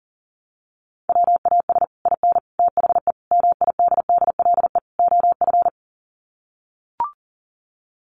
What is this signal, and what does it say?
Morse code, heard as a keyed tone, about 40 wpm, 700 Hz. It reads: WAS IN THE MIDDLE OF
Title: Sets of 5 Words Encore 40wpm